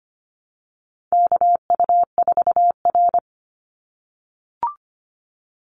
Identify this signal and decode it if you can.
Morse code, heard as a keyed tone, about 25 wpm, 700 Hz. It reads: KU4R
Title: NA SOTA Chaser Callsigns 25wpm